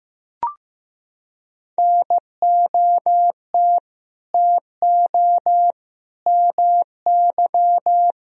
Morse code, heard as a keyed tone, about 15 wpm, 700 Hz. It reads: NOT TO MY
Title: Sentences from Top 1000 Words - Review Part 3 - 15wpm